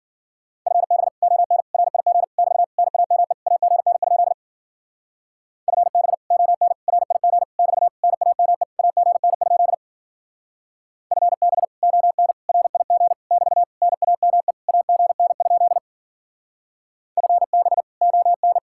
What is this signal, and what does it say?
Morse code, heard as a keyed tone, about 50 wpm, 700 Hz. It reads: FB ON RIG <BT> NAME AGN? FB ON RIG <BT> NAME AGN? FB ON RIG <BT> NAME AGN? FB ON